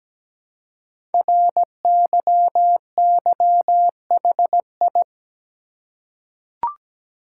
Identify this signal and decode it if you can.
Morse code, heard as a keyed tone, about 17 wpm, 700 Hz. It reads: RYYHI
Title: Code Groups - 5 Letters - 17wpm